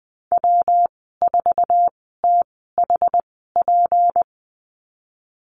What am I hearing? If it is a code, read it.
Morse code, heard as a keyed tone, about 20 wpm, spaced as for 14 wpm, 700 Hz. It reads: W4THP
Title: 186 Lesson 30g - Letter b - Call Signs 20wpm